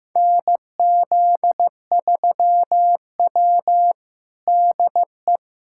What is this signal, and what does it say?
Morse code, heard as a keyed tone, about 16 wpm, 700 Hz. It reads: NZ3W DE
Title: USA Callsign DE Callsign - 4 Characters - 15wpm